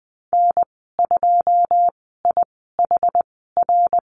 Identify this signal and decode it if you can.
Morse code, heard as a keyed tone, about 20 wpm, spaced as for 15 wpm, 700 Hz. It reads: N2IHR
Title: 082 Lesson 14g - Number 2 - Call Signs 20wpm